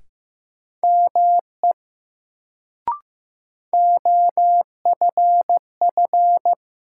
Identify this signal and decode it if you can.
Morse code, heard as a keyed tone, about 15 wpm, 700 Hz. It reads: ME OFF
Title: Sentences from Top 200 Words - New Word Drill - 15wpm